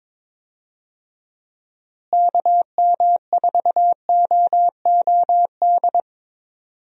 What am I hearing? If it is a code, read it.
Morse code, heard as a keyed tone, about 21 wpm, 700 Hz. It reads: KM4OOD
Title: USA Call Signs - 6 Characters - Repeated 3x - 22wpm